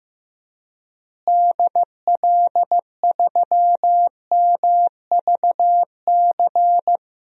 Morse code, heard as a keyed tone, about 15 wpm, 700 Hz. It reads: DL3MVC
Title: DXCC-100 Call Signs - 6 Characters - Repeated 3x - 15wpm